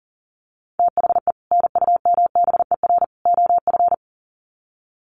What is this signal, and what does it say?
Morse code, heard as a keyed tone, about 40 wpm, 700 Hz. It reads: THE NUMBER OF
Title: Sets of 3 Words 40wpm